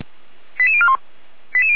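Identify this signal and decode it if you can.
Five-tone selective call (ZVEI), heard as quick runs of five tones, tones of about 70 ms. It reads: 88041
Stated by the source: ZVEI five-tone call, address 89084, received on an SDR.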